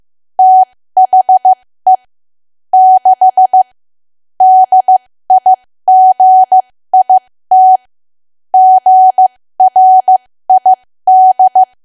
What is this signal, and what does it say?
Morse code, heard as a keyed tone, about 15 wpm, spaced as for 13 wpm, 750 Hz. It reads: THE 6 DIGIT GRID
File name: cwa / W1AW-Level2 / 13WPM